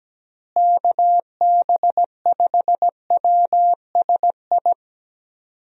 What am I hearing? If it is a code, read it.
Morse code, heard as a keyed tone, about 17 wpm, 700 Hz. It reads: KB5WSI
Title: USA Call Signs - 6 Characters - Repeated 3x - 17wpm